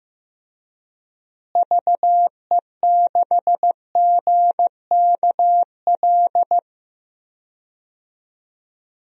Morse code, heard as a keyed tone, about 15 wpm, 700 Hz. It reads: VE6GKL
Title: DXCC-100 Call Signs - 6 Characters - Repeated 2x - 15wpm